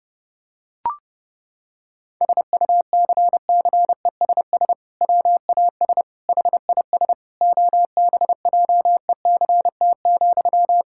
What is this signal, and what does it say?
Morse code, heard as a keyed tone, about 31 wpm, 700 Hz. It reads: SUCCESS WAS HIS OBJECT,
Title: Sentences from Top 1000 Words - Review Part 6 - 30wpm